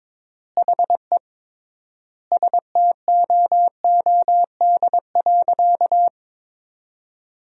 Morse code, heard as a keyed tone, about 22 wpm, 700 Hz. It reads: HE STOOD.
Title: 3x Word-Spacing - Sentences from Top 500 Words - 22wpm